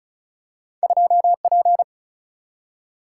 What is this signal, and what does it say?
Morse code, heard as a keyed tone, about 36 wpm, 700 Hz. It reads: 2P
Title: Single-Letter Single-Number 35wpm